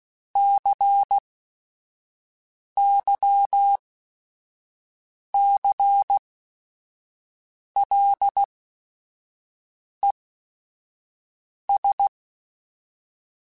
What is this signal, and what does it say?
Morse code, heard as a keyed tone, about 16 wpm, spaced as for 5 wpm, 800 Hz. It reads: CYCLES
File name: Technical Subjects / Technical Subjects-mp3s / Reactance @5